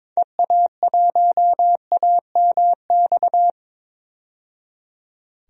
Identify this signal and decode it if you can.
Morse code, heard as a keyed tone, about 22 wpm, 700 Hz. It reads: EA1AMX
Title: DXCC-100 Call Signs - 6 Characters - 22wpm